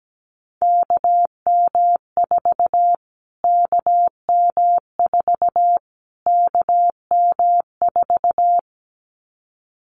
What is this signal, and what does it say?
Morse code, heard as a keyed tone, about 17 wpm, 700 Hz. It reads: KM4 KM4 KM4
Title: Callsign Prefixes for Top 25 DXCC - Repeated 3x - 17wpm